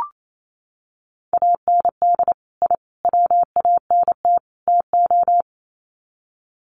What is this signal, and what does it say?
Morse code, heard as a keyed tone, about 28 wpm, 700 Hz. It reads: AND I WANT TO
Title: Sets of 4 Words 28wpm